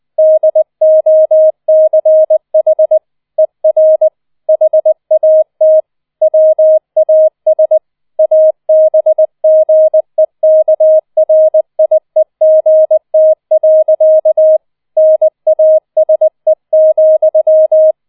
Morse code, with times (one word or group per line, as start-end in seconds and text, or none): 0.2-3.0 DOCH
3.4-4.1 ER
4.5-5.8 HAT
6.2-7.8 WAS
8.2-14.6 ABGEKRIEGT.
15.0-17.9 NASE,